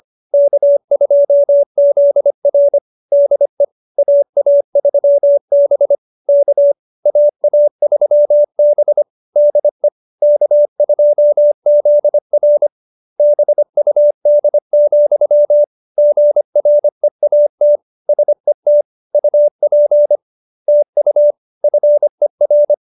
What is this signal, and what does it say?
Morse code, heard as a keyed tone, about 25 wpm, 590 Hz. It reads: K2ZR DE AA3B K AA3B DE K2ZR BUD, GREAT SET UP TU FER
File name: QSO302 25